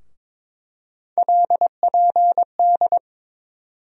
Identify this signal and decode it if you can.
Morse code, heard as a keyed tone, about 22 wpm, 700 Hz. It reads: LPD
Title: USA Call Sign Postfixes - 3 Letters 22wpm